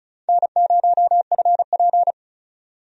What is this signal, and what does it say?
Morse code, heard as a keyed tone, about 36 wpm, 700 Hz. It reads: N0FP
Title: USA Call Signs - 4 Characters - 35wpm